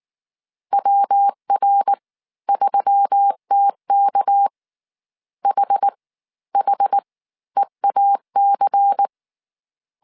Morse code, heard as a keyed tone, about 19 wpm, spaced as for 14 wpm, 800 Hz. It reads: WR3TK HHEAC
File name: CW / DET / 1 6 1500C S C 18wpm 2356